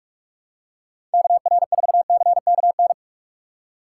Morse code, heard as a keyed tone, about 48 wpm, 700 Hz. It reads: KR4CKN
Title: USA Call Signs - 6 Characters - 6x Character Spacing - 45wpm